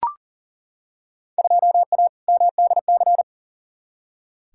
Morse code, heard as a keyed tone, about 37 wpm, 700 Hz. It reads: 2A MDC
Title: Field Day Exchanges - 40wpm